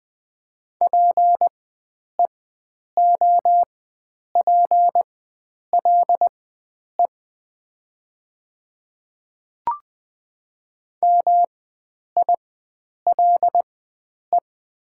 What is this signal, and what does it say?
Morse code, heard as a keyed tone, about 20 wpm, spaced as for 9 wpm, 700 Hz. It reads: PEOPLE MILE
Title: 4x Character Spacing - Top 300 Words - 20wpm